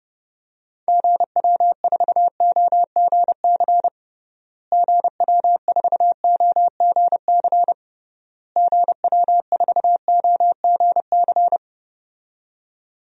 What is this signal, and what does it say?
Morse code, heard as a keyed tone, about 32 wpm, 700 Hz. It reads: GW4OGC GW4OGC GW4OGC
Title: DXCC-100 Call Signs - 6 Characters - Repeated 3x - 30wpm